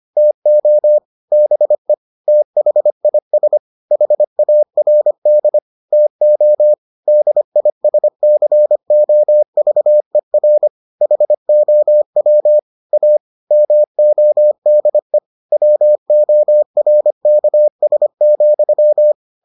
Morse code, heard as a keyed tone, about 25 wpm, 600 Hz. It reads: TO BE THIS HARD TO DISCOVER HOW A MODE WORKS,